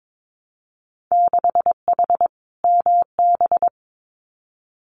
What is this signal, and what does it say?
Morse code, heard as a keyed tone, about 22 wpm, 700 Hz. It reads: 6H MB
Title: Winter Field Day Exchanges Encore 22wpm